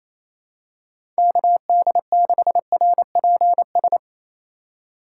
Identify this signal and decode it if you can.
Morse code, heard as a keyed tone, about 28 wpm, 700 Hz. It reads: KD6RPS
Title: USA Call Signs - 6 Characters - 3x Character Spacing - 28wpm